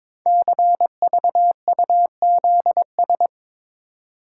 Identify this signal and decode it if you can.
Morse code, heard as a keyed tone, about 22 wpm, 700 Hz. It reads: CVUZS
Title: Code Groups - 5 Letters - 22wpm